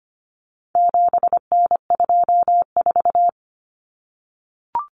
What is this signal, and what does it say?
Morse code, heard as a keyed tone, about 25 wpm, 700 Hz. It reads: 7N24
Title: Common Cut Numbers - 4-Digit 25wpm